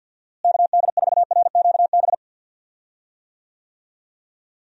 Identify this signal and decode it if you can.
Morse code, heard as a keyed tone, about 52 wpm, 700 Hz. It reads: KD4RQB
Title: USA Call Signs - 6 Characters - Repeated 3x - 50wpm